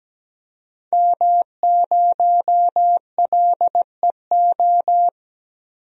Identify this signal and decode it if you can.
Morse code, heard as a keyed tone, about 17 wpm, 700 Hz. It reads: M0LEO
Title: DXCC-100 Call Signs - 5 Characters - 17wpm